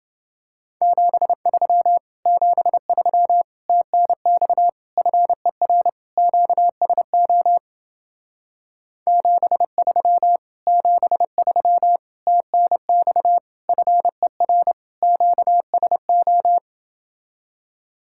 Morse code, heard as a keyed tone, about 29 wpm, 700 Hz. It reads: 73 73 TNX FER QSO 73 73 TNX FER QSO